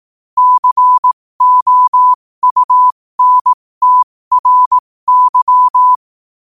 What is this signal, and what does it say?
Morse code, heard as a keyed tone, about 18 wpm, spaced as for 15 wpm, 1000 Hz. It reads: COUNTRY
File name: qotdfiles / 15wpm